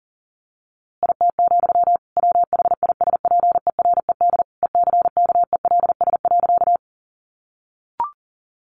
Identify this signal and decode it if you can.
Morse code, heard as a keyed tone, about 37 wpm, 700 Hz. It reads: IT, WHISPERED ECKELS.
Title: A Sound Of Thunder 40wpm